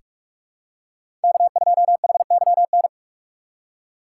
Morse code, heard as a keyed tone, about 42 wpm, 700 Hz. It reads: K2HYN
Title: USA Call Signs - 5 Characters - 6x Character Spacing - 45wpm